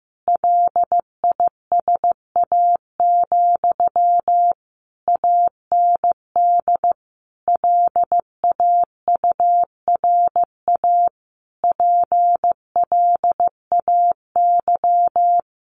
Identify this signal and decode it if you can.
Morse code, heard as a keyed tone, about 15 wpm, 700 Hz. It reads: LISA, AND LAURA PLAY
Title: Additional Easy Phrases 15wpm